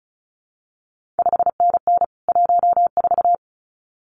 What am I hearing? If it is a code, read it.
Morse code, heard as a keyed tone, about 36 wpm, 700 Hz. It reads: 5NN 14
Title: CQ WW DX Contest Exchange - 35wpm